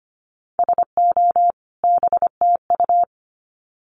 Morse code, heard as a keyed tone, about 25 wpm, 700 Hz. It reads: SO BTU
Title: Common QSO Phrases 25wpm